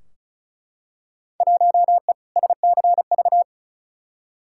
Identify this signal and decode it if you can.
Morse code, heard as a keyed tone, about 34 wpm, 700 Hz. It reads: 1E SCV
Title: Field Day Exchanges - 35wpm